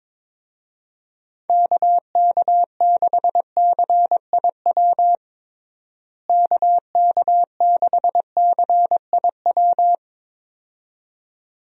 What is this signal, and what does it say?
Morse code, heard as a keyed tone, about 22 wpm, 700 Hz. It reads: KK6CIW KK6CIW
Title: USA Call Signs - 6 Characters - Repeated 2x - 22wpm